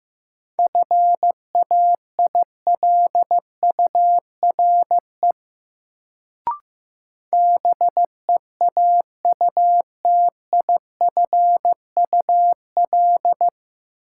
Morse code, heard as a keyed tone, about 15 wpm, 700 Hz. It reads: FAILURE BEAUTIFUL
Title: Sentences from Top 900 Words - New Word Drill - 15wpm